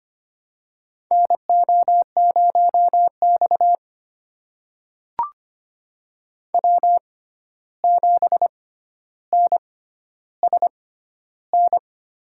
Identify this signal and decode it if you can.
Morse code, heard as a keyed tone, about 25 wpm, spaced as for 12 wpm, 700 Hz. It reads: NO0X W7NSN
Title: USA Call Signs - 6x Character Spacing - 25wpm